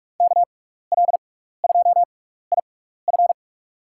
Morse code, heard as a keyed tone, about 42 wpm, spaced as for 15 wpm, 700 Hz. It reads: KL2IF
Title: USA Call Signs - 5 Characters - 6x Character Spacing - 45wpm